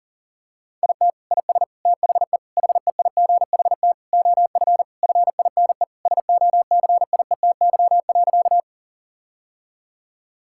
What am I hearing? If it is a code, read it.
Morse code, heard as a keyed tone, about 40 wpm, 700 Hz. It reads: IT IS THE HEIGHT OF FINE SOCIETY.
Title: Sentences from Top 1600 Words 40wpm